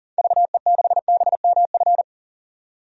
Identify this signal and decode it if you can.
Morse code, heard as a keyed tone, about 40 wpm, 700 Hz. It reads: VE6BMF
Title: DXCC-100 Call Signs - 6 Characters - 40wpm